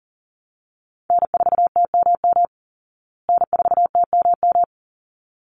Morse code, heard as a keyed tone, about 40 wpm, 700 Hz. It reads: N4TMM N4TMM
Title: Call Signs Repeated 3x - 40wpm